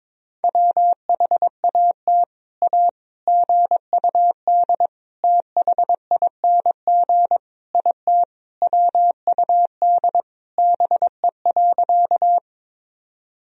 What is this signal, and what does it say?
Morse code, heard as a keyed tone, about 22 wpm, 700 Hz. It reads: WHAT A GUD THING IT WUD BE.
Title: Sentences from Top 200 Words and Abbreviations 22wpm